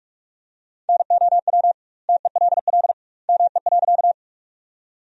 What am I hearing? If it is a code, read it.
Morse code, heard as a keyed tone, about 46 wpm, 700 Hz. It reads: NOW TELL ME.
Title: 2x Word-Spacing - Sentences from Top 200 Words - 45wpm